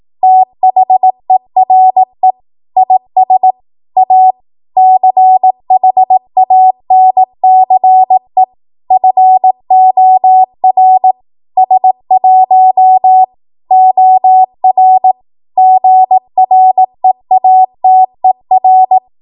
Morse code, heard as a keyed tone, about 18 wpm, 750 Hz. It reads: THERE IS A CHANCE FOR S1 OR GREATER
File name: Morse / 18WPM